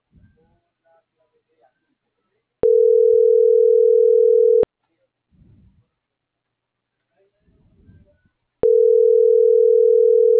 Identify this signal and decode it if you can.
Ringback tone (North America), 440 + 480 Hz.
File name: CENTRAL MABE / out antonio.garcia